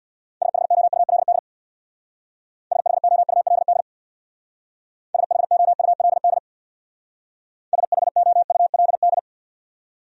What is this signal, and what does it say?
Morse code, heard as a keyed tone, about 58 wpm, 700 Hz. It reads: SHOULD SHOULD SHOULD SHOULD
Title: Speed-Racing Top 300 Words 50wpm